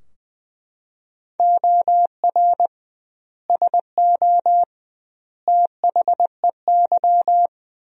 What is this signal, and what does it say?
Morse code, heard as a keyed tone, about 20 wpm, 700 Hz. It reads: OR SO THEY
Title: 2x Word-Spacing - Sentences from Top 100 Words - 20wpm